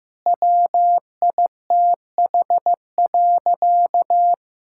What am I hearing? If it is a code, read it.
Morse code, heard as a keyed tone, about 15 wpm, 700 Hz. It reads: WITH.
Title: Sentences from Top 1000 Words - Review Part 6 - 15wpm